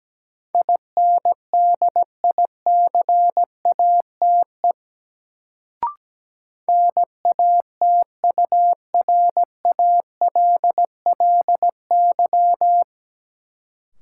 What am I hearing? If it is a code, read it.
Morse code, heard as a keyed tone, about 17 wpm, 700 Hz. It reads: INDICATE NATURALLY
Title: Sentences from Top 1300 Words - New Word Drill - 17wpm